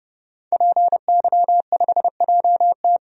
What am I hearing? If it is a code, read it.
Morse code, heard as a keyed tone, about 30 wpm, 700 Hz. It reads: PY5JT
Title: DXCC-100 Call Signs - 5 Characters - 30wpm